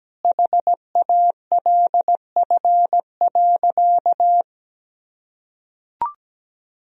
Morse code, heard as a keyed tone, about 17 wpm, 700 Hz. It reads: HALF.
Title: Sentences from Top 1000 Words - Review Part 4 - 17wpm